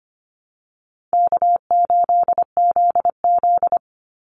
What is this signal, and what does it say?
Morse code, heard as a keyed tone, about 25 wpm, 700 Hz. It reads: K8ZZ
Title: Easy Call Signs Repeated 2x - 25wpm